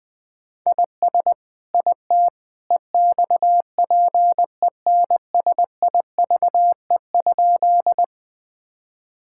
Morse code, heard as a keyed tone, about 19 wpm, 700 Hz. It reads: IS IT EXPENSIVE?
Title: Sentences from Top 1000 Words - Review Part 6 - 20wpm